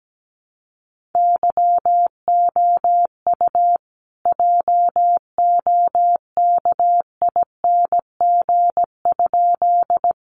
Morse code, heard as a keyed tone, about 17 wpm, 700 Hz. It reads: YOU JOKING?
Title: A Sound Of Thunder 17wpm